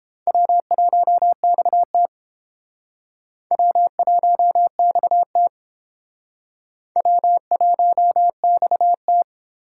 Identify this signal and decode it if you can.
Morse code, heard as a keyed tone, about 30 wpm, 700 Hz. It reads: W1XT W1XT W1XT